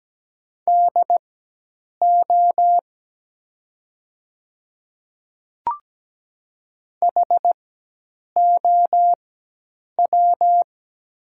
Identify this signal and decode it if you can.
Morse code, heard as a keyed tone, about 17 wpm, spaced as for 8 wpm, 700 Hz. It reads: DO HOW
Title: 4x Character Spacing - Top 100 Words - 17wpm